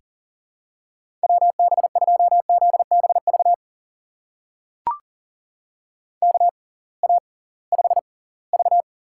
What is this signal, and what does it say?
Morse code, heard as a keyed tone, about 40 wpm, spaced as for 23 wpm, 700 Hz. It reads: WB2ZBV KA5V
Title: USA Call Signs - 6 Characters - 6x Character Spacing - 40wpm